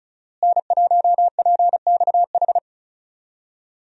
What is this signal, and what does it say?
Morse code, heard as a keyed tone, about 36 wpm, 700 Hz. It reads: N1PXH
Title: USA Call Signs - 5 Characters - Repeated 3x - 35wpm